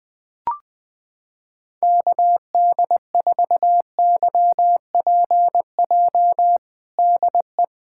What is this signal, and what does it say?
Morse code, heard as a keyed tone, about 21 wpm, 700 Hz. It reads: KD4YPJ DE
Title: USA Callsign DE Callsign - 6 Characters - 20wpm